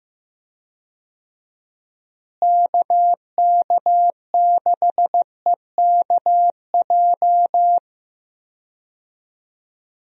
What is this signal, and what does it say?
Morse code, heard as a keyed tone, about 15 wpm, 700 Hz. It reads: KK6EKJ
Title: USA Call Signs - 6 Characters - Repeated 3x - 15wpm